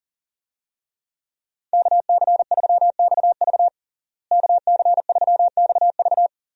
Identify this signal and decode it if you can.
Morse code, heard as a keyed tone, about 40 wpm, 700 Hz. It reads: KC3XV KC3XV
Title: USA Call Signs - 5 Characters - Repeated 2x - 40wpm